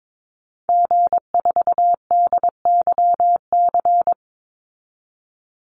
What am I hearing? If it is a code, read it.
Morse code, heard as a keyed tone, about 22 wpm, 700 Hz. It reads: G4DYC
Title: Call Signs Repeated 3x - 22wpm